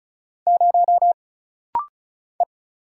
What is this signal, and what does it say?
Morse code, heard as a keyed tone, about 36 wpm, 700 Hz. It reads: 0 E
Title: Single Letter-Number - Rapid-Fire 35wpm